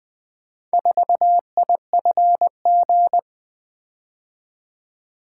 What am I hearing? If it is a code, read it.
Morse code, heard as a keyed tone, about 20 wpm, 700 Hz. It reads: 4IFG
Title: Four Letter-Number Combinations - 20wpm